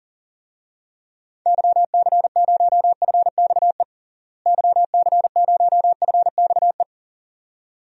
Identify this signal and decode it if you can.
Morse code, heard as a keyed tone, about 40 wpm, 700 Hz. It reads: YC0FXE YC0FXE
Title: DXCC-100 Call Signs - 6 Characters - Repeated 2x - 40wpm